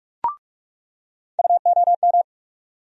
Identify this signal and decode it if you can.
Morse code, heard as a keyed tone, about 46 wpm, 700 Hz. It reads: UOM
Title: USA Call Sign Postfixes - 3 Letters 45wpm